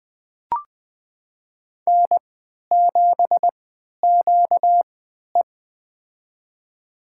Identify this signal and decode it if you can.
Morse code, heard as a keyed tone, about 20 wpm, spaced as for 11 wpm, 700 Hz. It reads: N7QE
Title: USA Call Signs - 4 Characters - 3x Character Spacing - 20wpm